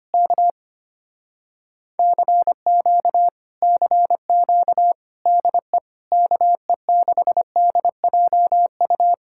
Morse code, heard as a keyed tone, about 25 wpm, 700 Hz. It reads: K CQ CQ DE KE6DJU
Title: Speed-Racing Common QSO Phrases Encore - 25wpm